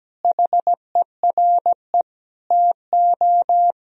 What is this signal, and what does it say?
Morse code, heard as a keyed tone, about 17 wpm, 700 Hz. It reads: HERE TO
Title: Sentences from Top 1000 Words - Review Part 1 - 17wpm